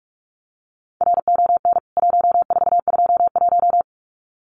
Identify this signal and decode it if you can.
Morse code, heard as a keyed tone, about 44 wpm, 700 Hz. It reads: RON 1421
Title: CWT Exchange 45wpm